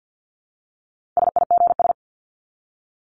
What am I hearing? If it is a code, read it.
Morse code, heard as a keyed tone, about 48 wpm, 700 Hz. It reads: SIGS
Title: Abbreviations 50wpm